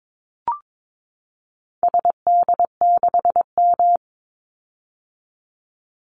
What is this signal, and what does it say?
Morse code, heard as a keyed tone, about 22 wpm, 700 Hz. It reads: SD6M
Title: Call Signs - 22wpm